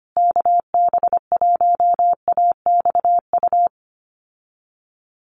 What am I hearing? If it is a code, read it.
Morse code, heard as a keyed tone, about 25 wpm, 700 Hz. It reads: KB1AXU
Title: USA Call Signs Repeated 3x 25wpm